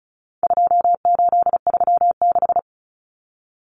Morse code, heard as a keyed tone, about 34 wpm, 700 Hz. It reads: 2836
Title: Common Cut Numbers - 4-Digit 35wpm